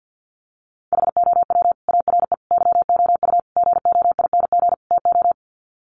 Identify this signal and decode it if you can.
Morse code, heard as a keyed tone, about 50 wpm, 700 Hz. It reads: HOW ARE YOU GOING TO
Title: Sets of 5 Words 50wpm